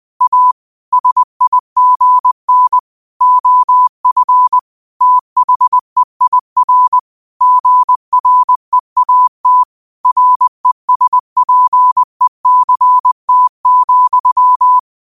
Morse code, heard as a keyed tone, about 20 wpm, 1000 Hz. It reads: A SIGN OF THEIR GREAT RESPECT,
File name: qotdfiles / 20wpm